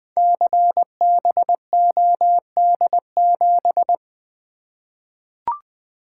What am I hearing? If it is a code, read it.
Morse code, heard as a keyed tone, about 20 wpm, 700 Hz. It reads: CBOD7
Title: Code Groups - 5 Letter:Numbers - 20wpm